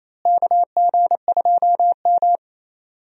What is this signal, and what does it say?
Morse code, heard as a keyed tone, about 28 wpm, 700 Hz. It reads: KG2M
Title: USA Call Signs - 4 Characters - Repeated 2x - 28wpm